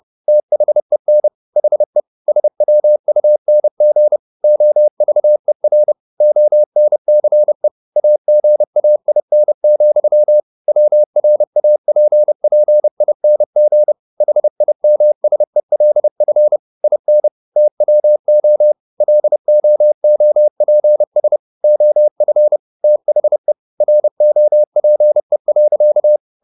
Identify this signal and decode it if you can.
Morse code, heard as a keyed tone, about 30 wpm, 600 Hz. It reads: THEN HE SWUNG OVER ONCE AGAIN, WRAPPING HIMSELF IN TWO LOOPS OF THE ROPE.